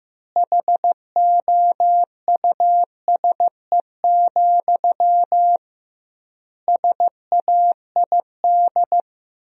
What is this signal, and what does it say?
Morse code, heard as a keyed tone, about 16 wpm, 700 Hz. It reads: HOUSE, SAID
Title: 2x Word-Spacing - Sentences from Top 400 Words - 15wpm